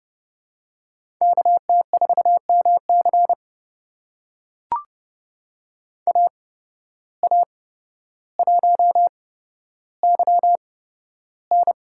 Morse code, heard as a keyed tone, about 30 wpm, spaced as for 13 wpm, 700 Hz. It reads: KT4MC AA1YN
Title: USA Call Signs - 5 Characters - 8x Character Spacing - 30wpm